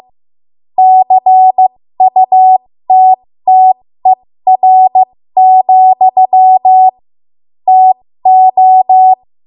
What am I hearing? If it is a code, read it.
Morse code, heard as a keyed tone, about 15 wpm, spaced as for 13 wpm, 750 Hz. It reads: CUTTER, TO
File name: Morse / 13WPM